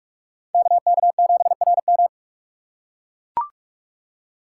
Morse code, heard as a keyed tone, about 42 wpm, 700 Hz. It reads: KK7RM
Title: USA Call Signs - 5 Characters - 4x Character Spacing - 45wpm